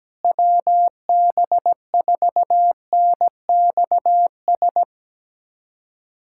Morse code, heard as a keyed tone, about 17 wpm, 700 Hz. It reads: WB4NXS
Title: K1USN SST QSO Practice - 17wpm